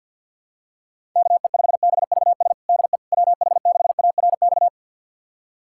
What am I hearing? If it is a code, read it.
Morse code, heard as a keyed tone, about 52 wpm, 700 Hz. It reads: KE5BVS DE WH6ARX
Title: USA Callsign DE Callsign - 6 Characters - 50wpm